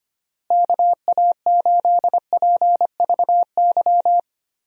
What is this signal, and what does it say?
Morse code, heard as a keyed tone, about 25 wpm, 700 Hz. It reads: KA8PVY
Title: USA Call Signs - 6 Characters - Repeated 3x - 25wpm